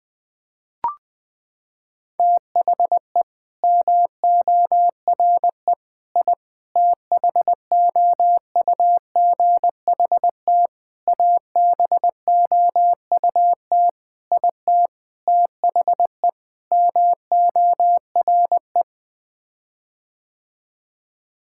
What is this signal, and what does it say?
Morse code, heard as a keyed tone, about 21 wpm, 700 Hz. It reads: THE MORE I THOUGHT ABOUT IT THE MORE
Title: Sets of 8 Words - 20wpm